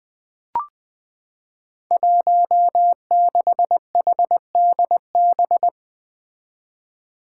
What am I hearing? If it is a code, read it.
Morse code, heard as a keyed tone, about 21 wpm, 700 Hz. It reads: 16HDB
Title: Five Challenging Letter-Number Combinations 20wpm